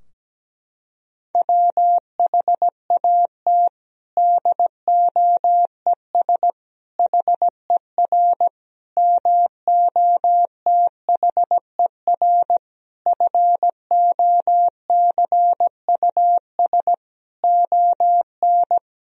Morse code, heard as a keyed tone, about 17 wpm, 700 Hz. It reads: WHAT DOES HER MOTHER FOCUS ON
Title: Sentences from Top 1000 Words - Review Part 4 - 17wpm